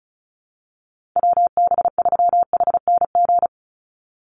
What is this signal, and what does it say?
Morse code, heard as a keyed tone, about 34 wpm, 700 Hz. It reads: WB3HNG
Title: USA Call Signs 35wpm